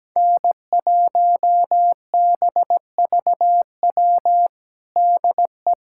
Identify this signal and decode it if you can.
Morse code, heard as a keyed tone, about 17 wpm, 700 Hz. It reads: N1BVW DE
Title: USA Callsign DE Callsign - 5 Characters - 17wpm